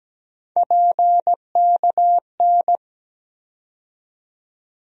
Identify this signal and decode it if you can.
Morse code, heard as a keyed tone, about 17 wpm, 700 Hz. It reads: PKN
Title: USA Call Sign Postfixes - 3 Letters 17wpm